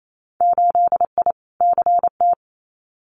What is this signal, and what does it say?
Morse code, heard as a keyed tone, about 28 wpm, 700 Hz. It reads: 8I CT
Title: Winter Field Day Exchanges Encore 28wpm